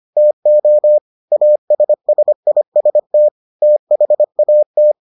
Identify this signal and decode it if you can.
Morse code, heard as a keyed tone, about 25 wpm, 600 Hz. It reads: TO ASSIST THAT